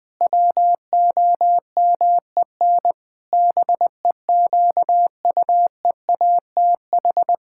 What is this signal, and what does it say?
Morse code, heard as a keyed tone, about 20 wpm, 700 Hz. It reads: WOMEN BEQUEATH
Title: Pangram Challenge v2 - 20wpm